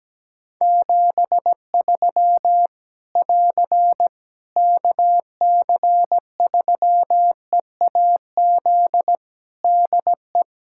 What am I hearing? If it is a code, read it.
Morse code, heard as a keyed tone, about 17 wpm, 700 Hz. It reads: 73 <AR> KC3EAZ DE
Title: Practice QSO - Exchange 4 - Ending - 17wpm